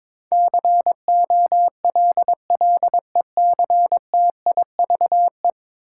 Morse code, heard as a keyed tone, about 22 wpm, 700 Hz. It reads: COLLECTIVE
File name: practice / 22wpm / 10 Letter Words - First 3 Letters Given - 22wpm